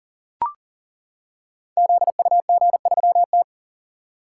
Morse code, heard as a keyed tone, about 42 wpm, 700 Hz. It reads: ZUG3T
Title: Code Groups - 5 Letter:Numbers - 40wpm